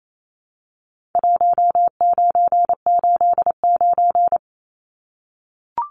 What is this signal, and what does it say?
Morse code, heard as a keyed tone, about 28 wpm, 700 Hz. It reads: 1989
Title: Numbers - 4-Digit 28wpm